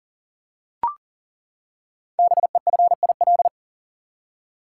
Morse code, heard as a keyed tone, about 42 wpm, 690 Hz. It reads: BEFIL
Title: Code Groups - 5 Letters - 40wpm